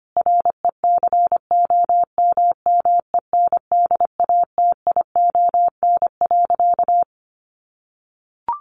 Morse code, heard as a keyed tone, about 25 wpm, 700 Hz. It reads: RECOMMENDATION.
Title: Sentences from Top 1400 Words 25wpm